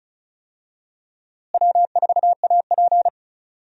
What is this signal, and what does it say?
Morse code, heard as a keyed tone, about 36 wpm, 700 Hz. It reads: W4AP
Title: USA Call Signs - 4 Characters - Repeated 3x - 35wpm